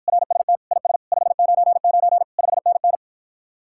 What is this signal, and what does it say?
Morse code, heard as a keyed tone, about 54 wpm, 700 Hz. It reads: RST IS 599 5NN